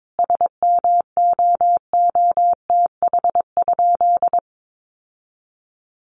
Morse code, heard as a keyed tone, about 22 wpm, 700 Hz. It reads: SMOOTH?
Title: Sentences from Top 1400 Words 22wpm